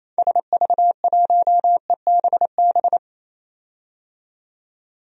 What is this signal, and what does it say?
Morse code, heard as a keyed tone, about 28 wpm, 700 Hz. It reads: SV1EBB
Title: DXCC-100 Call Signs - 6 Characters - 28wpm